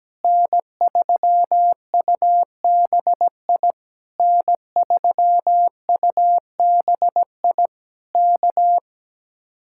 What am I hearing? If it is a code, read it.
Morse code, heard as a keyed tone, about 17 wpm, 700 Hz. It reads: N3UBI N3UBI K
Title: Speed-Racing Common QSO Phrases - 17wpm